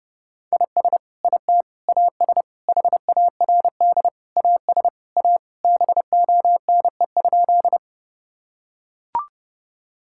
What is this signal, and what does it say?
Morse code, heard as a keyed tone, about 30 wpm, 700 Hz. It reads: IS IT AS HARD AS A BONE?
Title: Sentences from Top 1600 Words 30wpm